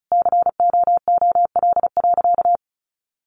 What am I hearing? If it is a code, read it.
Morse code, heard as a keyed tone, about 34 wpm, 700 Hz. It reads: COOL.
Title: Sentences from Top 1200 Words 35wpm